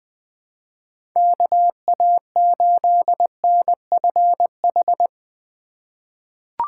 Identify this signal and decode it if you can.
Morse code, heard as a keyed tone, about 20 wpm, 700 Hz. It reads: KA8NFH
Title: USA Call Signs - 6 Characters - 4x Character Spacing - 20wpm